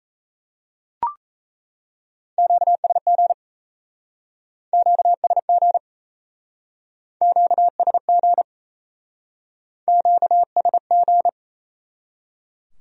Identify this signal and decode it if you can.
Morse code, heard as a keyed tone, about 37 wpm, 700 Hz. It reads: QSG QSG QSG QSG